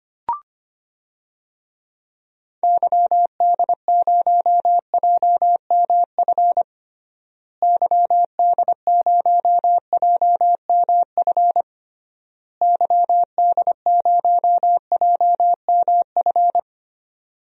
Morse code, heard as a keyed tone, about 25 wpm, 700 Hz. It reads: YD0JMF YD0JMF YD0JMF
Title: DXCC-100 Call Signs - 6 Characters - Repeated 3x - 25wpm